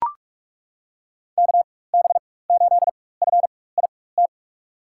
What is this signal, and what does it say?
Morse code, heard as a keyed tone, about 42 wpm, spaced as for 21 wpm, 700 Hz. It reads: KB8FIT
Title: USA Call Signs - 6 Characters - 4x Character Spacing - 45wpm